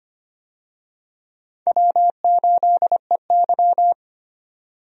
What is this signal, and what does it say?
Morse code, heard as a keyed tone, about 25 wpm, 700 Hz. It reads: W8EY
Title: USA Call Signs - 4 Characters - 25wpm